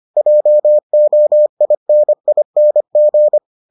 Morse code, heard as a keyed tone, about 25 wpm, 600 Hz. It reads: JOINING